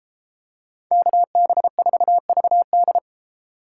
Morse code, heard as a keyed tone, about 32 wpm, 700 Hz. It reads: KB4VD